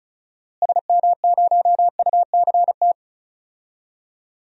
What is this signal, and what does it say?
Morse code, heard as a keyed tone, about 36 wpm, 700 Hz. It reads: SM0UCT
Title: DXCC-100 Call Signs - 6 Characters - Repeated 3x - 35wpm